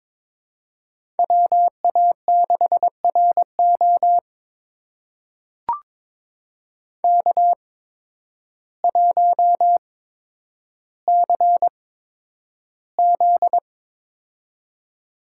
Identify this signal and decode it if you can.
Morse code, heard as a keyed tone, about 22 wpm, spaced as for 10 wpm, 700 Hz. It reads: WA6RO K1CZ
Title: USA Call Signs - 8x Character Spacing - 22wpm